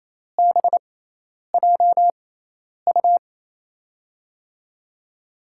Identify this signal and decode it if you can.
Morse code, heard as a keyed tone, about 28 wpm, spaced as for 10 wpm, 700 Hz. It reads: BJU
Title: USA Call Signs - 6 Characters - 6x Character Spacing - 28wpm